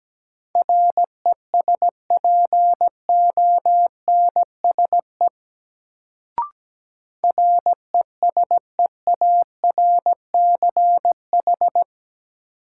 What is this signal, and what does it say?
Morse code, heard as a keyed tone, about 17 wpm, 700 Hz. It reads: RESPONSE RESEARCH
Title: Sentences from Top 1400 Words - New Word Drill Alt - 17wpm